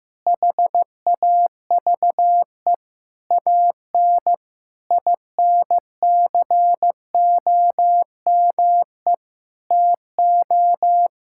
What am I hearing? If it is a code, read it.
Morse code, heard as a keyed tone, about 15 wpm, 700 Hz. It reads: HAVE AN INCOME TO
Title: Sentences from Top 1000 Words - Review Part 5 - 15wpm